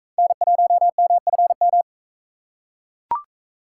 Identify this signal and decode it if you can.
Morse code, heard as a keyed tone, about 37 wpm, 700 Hz. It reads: N1MFM